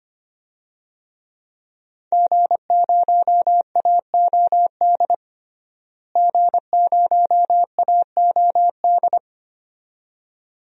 Morse code, heard as a keyed tone, about 25 wpm, 700 Hz. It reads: G0AOD G0AOD
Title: DXCC-100 Call Signs - 5 Characters - Repeated 2x - 25wpm